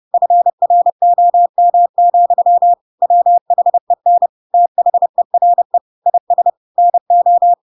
Morse code, heard as a keyed tone, about 30 wpm, 700 Hz. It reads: FROM, WHEN THERE IS NO
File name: Pinocchio / Pinocchio ch1 30 30 ik1hge